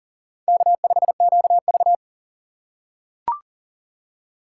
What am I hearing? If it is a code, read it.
Morse code, heard as a keyed tone, about 40 wpm, 700 Hz. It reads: K5QV
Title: USA Call Signs - 4x Character Spacing - 40wpm